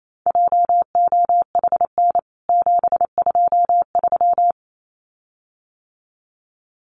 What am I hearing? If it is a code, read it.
Morse code, heard as a keyed tone, about 28 wpm, 700 Hz. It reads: JOHN 723
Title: CWT Exchange 28wpm